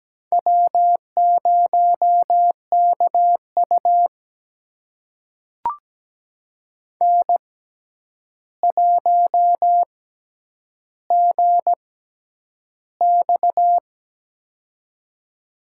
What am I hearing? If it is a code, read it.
Morse code, heard as a keyed tone, about 17 wpm, spaced as for 9 wpm, 700 Hz. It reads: W0KU N1GX
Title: USA Call Signs - 6x Character Spacing - 17wpm